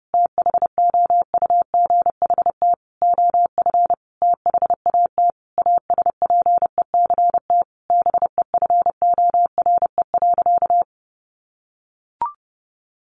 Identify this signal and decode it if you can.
Morse code, heard as a keyed tone, about 28 wpm, 700 Hz. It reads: THOUGHT OF THAT ASPECT BEFORE.
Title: Sentences from Top 1300 Words 30wpm